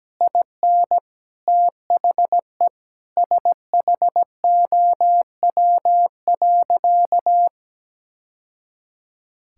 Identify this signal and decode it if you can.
Morse code, heard as a keyed tone, about 17 wpm, 700 Hz. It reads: IN THE SHOW.
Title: Sentences from Top 1000 Words - Review Part 6 - 17wpm